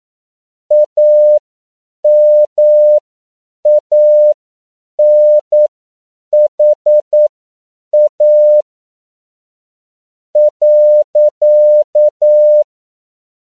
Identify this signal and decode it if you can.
Morse code, heard as a keyed tone, about 9 wpm, spaced as for 7 wpm, 600 Hz. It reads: AMANHA .